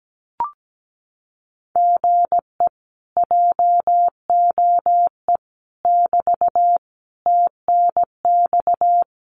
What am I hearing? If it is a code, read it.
Morse code, heard as a keyed tone, about 17 wpm, 700 Hz. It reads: GE JOE <BT> TNX
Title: Common QSO Phrases Encore 17wpm